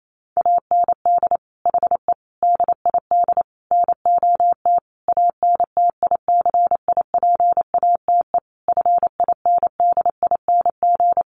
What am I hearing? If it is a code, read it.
Morse code, heard as a keyed tone, about 28 wpm, 700 Hz. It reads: AND HE DID NOT ANTICIPATE FINDING
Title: Sentences from Top 1500 Words 28wpm